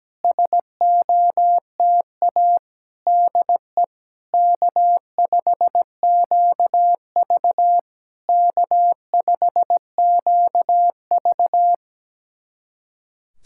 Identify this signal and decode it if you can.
Morse code, heard as a keyed tone, about 17 wpm, 700 Hz. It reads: SOTA DE K5QV K5QV
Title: SOTA QSO Practice 17wpm